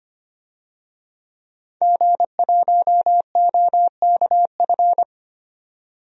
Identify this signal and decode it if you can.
Morse code, heard as a keyed tone, about 25 wpm, 700 Hz. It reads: G1OKF
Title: DXCC-100 Call Signs - 5 Characters - 25wpm